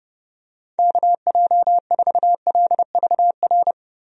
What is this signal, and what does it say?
Morse code, heard as a keyed tone, about 30 wpm, 700 Hz. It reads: KJ4LVR
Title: USA Call Signs - 6 Characters - 30wpm